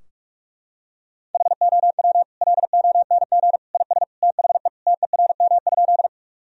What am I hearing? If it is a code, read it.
Morse code, heard as a keyed tone, about 41 wpm, 700 Hz. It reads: HOW LONG IS THE TERM?
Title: Sentences from Top 1000 Words - Review Part 4 - 45wpm